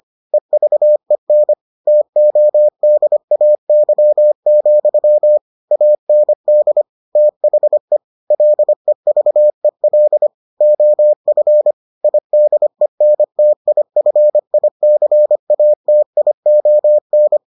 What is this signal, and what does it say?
Morse code, heard as a keyed tone, about 25 wpm, 600 Hz. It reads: EVEN TODAY, AND THE LEVEL OF IDENTIFICATION